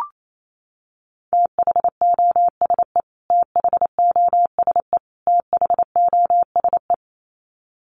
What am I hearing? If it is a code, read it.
Morse code, heard as a keyed tone, about 28 wpm, 700 Hz. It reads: THOSE THOSE THOSE
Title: Top 500 Words Repeated 28wpm